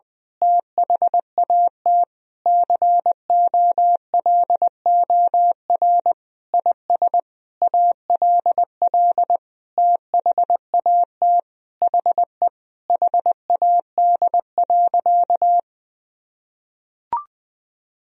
Morse code, heard as a keyed tone, about 20 wpm, 700 Hz. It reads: THAT COLOR IS ALL THAT HE HAD.
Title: Sentences from Top 1000 Words - Review Part 2 - 20wpm